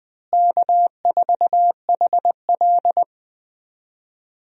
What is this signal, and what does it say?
Morse code, heard as a keyed tone, about 21 wpm, 700 Hz. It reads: K4HL
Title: USA Call Signs - 4 Characters - Repeated 3x - 20wpm